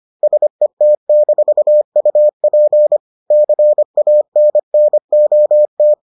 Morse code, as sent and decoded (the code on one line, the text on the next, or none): ... - -....- ..- .--. / -.-. .- -. -. --- -
SET-UP CANNOT